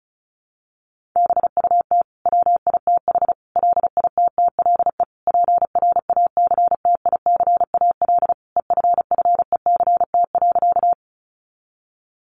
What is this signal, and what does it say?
Morse code, heard as a keyed tone, about 34 wpm, 700 Hz. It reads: BUT WITH LITTLE PRACTICAL EFFECT.
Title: Sentences from Top 1400 Words 35wpm